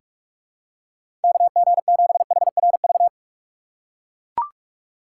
Morse code, heard as a keyed tone, about 41 wpm, 700 Hz. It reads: KG7HRV
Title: USA Call Signs - 6x Character Spacing - 45wpm